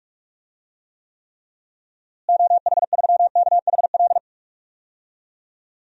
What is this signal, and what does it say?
Morse code, heard as a keyed tone, about 41 wpm, 700 Hz. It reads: OH3KHL
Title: DXCC-100 Call Signs - 6 Characters - Repeated 3x - 45wpm